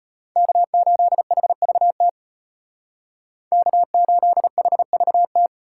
Morse code, heard as a keyed tone, about 33 wpm, 700 Hz. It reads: K8HVT K8HVT